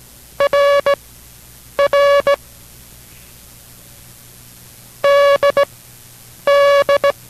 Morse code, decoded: RR DD

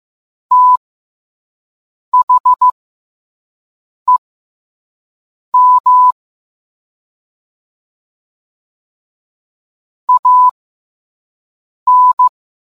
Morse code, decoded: THEM AN

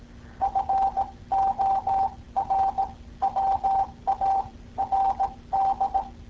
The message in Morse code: FORWARD

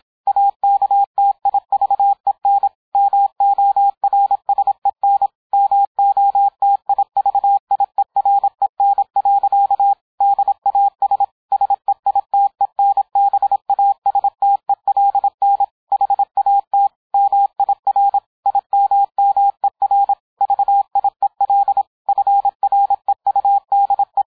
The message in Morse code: AKTIVEN MORSEN MOTIVIEREN. DAS SEITENBASTELN HAT MIR IMMER VIEL FREUDE

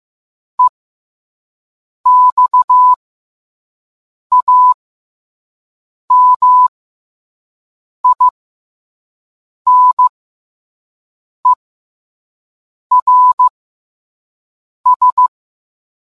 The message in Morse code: EXAMINERS

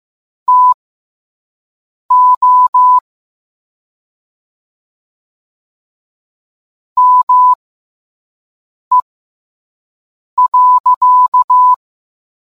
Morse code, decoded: TO ME.

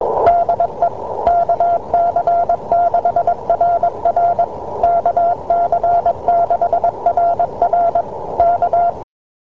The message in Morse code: DE KC6RR KC6RR K

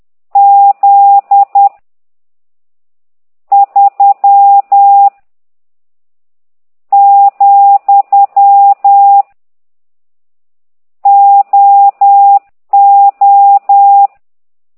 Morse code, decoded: Z 3 , O O

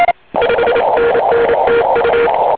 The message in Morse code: 5NN TU

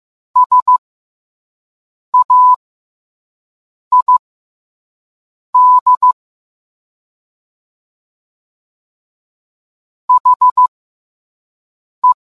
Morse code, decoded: SAID HE

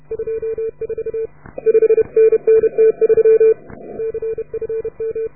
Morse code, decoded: 24 5NN T3 CFM

